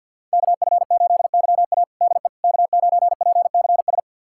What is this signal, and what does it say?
Morse code, heard as a keyed tone, about 52 wpm, 700 Hz. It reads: KF8YA DE K9PCS
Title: USA Callsign DE Callsign - 5 Characters - 50wpm